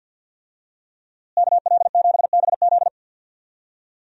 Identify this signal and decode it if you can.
Morse code, heard as a keyed tone, about 52 wpm, 700 Hz. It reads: KL7BZ
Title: DXCC-100 Call Signs - 5 Characters - Repeated 2x - 50wpm